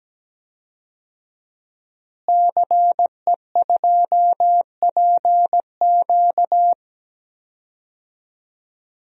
Morse code, heard as a keyed tone, about 17 wpm, 700 Hz. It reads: CE2PQ
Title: DXCC-100 Call Signs - 5 Characters - Repeated 2x - 17wpm